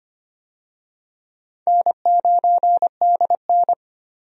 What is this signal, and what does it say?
Morse code, heard as a keyed tone, about 25 wpm, 700 Hz. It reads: N9DN
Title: USA Call Signs - 4 Characters - 25wpm